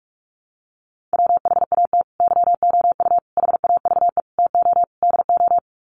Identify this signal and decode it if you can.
Morse code, heard as a keyed tone, about 44 wpm, 700 Hz. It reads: WHAT YOU HAVE TO DO
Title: Sets of 5 Words Encore 45wpm